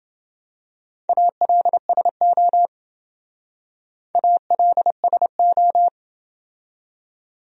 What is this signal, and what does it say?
Morse code, heard as a keyed tone, about 28 wpm, 700 Hz. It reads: ALSO ALSO